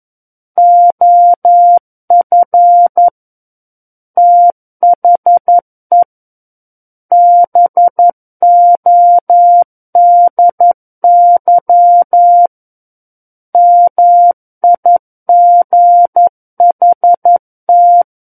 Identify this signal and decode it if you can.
Morse code, heard as a keyed tone, about 11 wpm, 700 Hz. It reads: OF THE BODY MIGHT
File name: 11 WPM